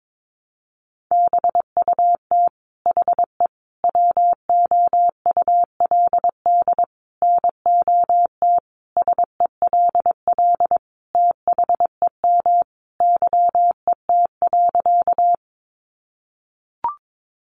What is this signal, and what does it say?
Morse code, heard as a keyed tone, about 22 wpm, 700 Hz. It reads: BUT HE WOULD NOT SELL THEM YET.
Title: Sentences from Top 1100 Words 22wpm